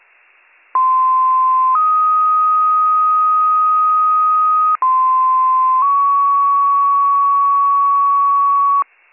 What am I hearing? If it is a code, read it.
Two-tone page, heard as a one-second tone, then a three-second tone, about 1040 then 1250 Hz, about 1040 then 1120 Hz. It is heard from 0.7 to 8.8 s.